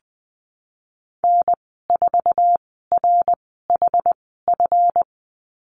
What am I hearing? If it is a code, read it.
Morse code, heard as a keyed tone, about 20 wpm, spaced as for 15 wpm, 700 Hz. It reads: N4RHF
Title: 147 Lesson 24g - Letter y - Call Signs 20wpm